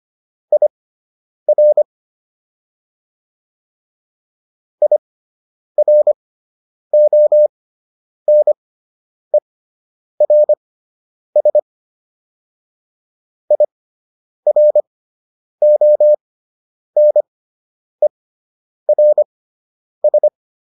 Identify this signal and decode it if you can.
Morse code, heard as a keyed tone, about 25 wpm, spaced as for 9 wpm, 600 Hz. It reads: IR IRONERS IRONERS